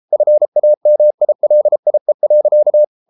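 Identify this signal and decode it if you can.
Morse code, heard as a keyed tone, about 33 wpm, 600 Hz. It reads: FAMILIE.
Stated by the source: Morse code at 33 wpm, QRQ.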